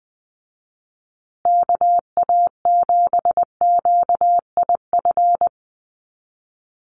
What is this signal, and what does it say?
Morse code, heard as a keyed tone, about 20 wpm, 700 Hz. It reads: KA7QIF
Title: USA Call Signs Repeated 2x 20wpm